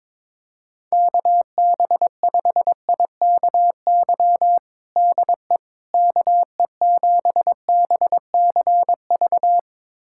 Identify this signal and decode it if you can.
Morse code, heard as a keyed tone, about 22 wpm, 700 Hz. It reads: KB5IKY DE KE7BCV
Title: USA Callsign DE Callsign - 6 Characters - 22wpm